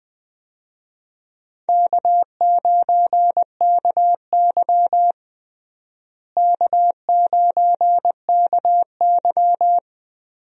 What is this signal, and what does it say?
Morse code, heard as a keyed tone, about 21 wpm, 700 Hz. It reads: K9KY K9KY
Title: USA Call Signs - 4 Characters - Repeated 2x - 20wpm